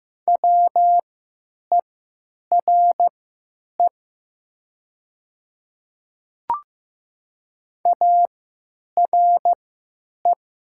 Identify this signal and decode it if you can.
Morse code, heard as a keyed tone, about 15 wpm, spaced as for 9 wpm, 700 Hz. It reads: WERE ARE
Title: 3x Character Spacing - Top 100 Words - 15wpm